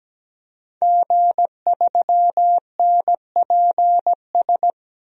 Morse code, heard as a keyed tone, about 17 wpm, 700 Hz. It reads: G3NPS
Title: DXCC-100 Call Signs - 5 Characters - 17wpm